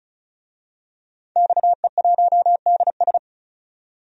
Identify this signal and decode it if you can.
Morse code, heard as a keyed tone, about 36 wpm, 700 Hz. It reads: XE1DS
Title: DXCC-100 Call Signs - 5 Characters - 35wpm